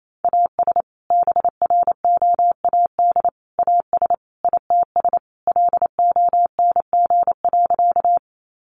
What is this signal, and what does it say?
Morse code, heard as a keyed tone, about 28 wpm, 700 Hz. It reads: AS BROAD AS ITS LONG.
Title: Sentences from Top 1000 Words 28wpm